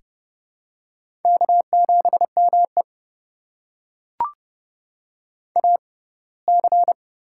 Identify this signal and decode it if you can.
Morse code, heard as a keyed tone, about 30 wpm, spaced as for 20 wpm, 700 Hz. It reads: K7ME AC
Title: USA Call Signs - 4 Characters - 6x Character Spacing - 30wpm